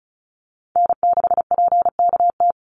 Morse code, heard as a keyed tone, about 34 wpm, 700 Hz. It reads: N6PKT
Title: NA SOTA Chaser Callsigns 35wpm